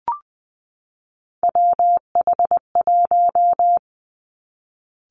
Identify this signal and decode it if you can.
Morse code, heard as a keyed tone, about 20 wpm, 700 Hz. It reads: WH1
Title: Three Challenging Letter-Number Combinations 20wpm